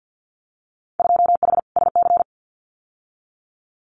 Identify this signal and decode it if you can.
Morse code, heard as a keyed tone, about 50 wpm, 700 Hz. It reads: 2H SC
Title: Winter Field Day Exchanges 50wpm